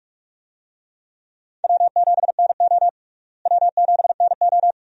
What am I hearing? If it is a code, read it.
Morse code, heard as a keyed tone, about 46 wpm, 700 Hz. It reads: W7NO W7NO
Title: USA Call Signs - 4 Characters - Repeated 2x - 45wpm